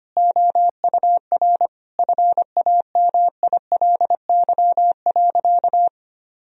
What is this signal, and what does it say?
Morse code, heard as a keyed tone, about 25 wpm, 700 Hz. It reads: OUR FAMILY.
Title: Sentences from Top 1000 Words - Review Part 4 - 25wpm